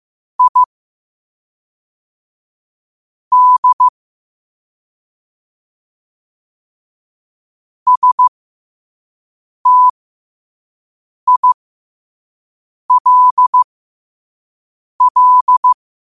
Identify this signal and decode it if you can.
Morse code, heard as a keyed tone, about 15 wpm, spaced as for 5 wpm, 1000 Hz. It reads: I D STILL
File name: qotdfiles / 5wpm